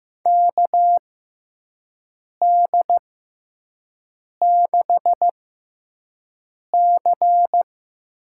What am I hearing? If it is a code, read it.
Morse code, heard as a keyed tone, about 15 wpm, spaced as for 5 wpm, 700 Hz. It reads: KD6C
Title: USA Call Signs - 6 Characters - 6x Character Spacing - 15wpm